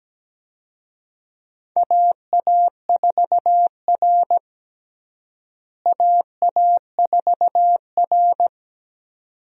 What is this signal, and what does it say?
Morse code, heard as a keyed tone, about 17 wpm, 700 Hz. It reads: AA4R AA4R
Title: USA Call Signs - 4 Characters - Repeated 2x - 17wpm